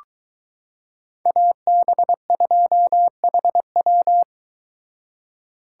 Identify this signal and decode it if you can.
Morse code, heard as a keyed tone, about 22 wpm, 700 Hz. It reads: AB2HW